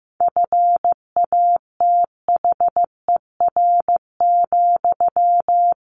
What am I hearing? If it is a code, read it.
Morse code, heard as a keyed tone, about 15 wpm, 700 Hz. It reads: FATHER,
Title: Sentences from Top 1100 Words 15wpm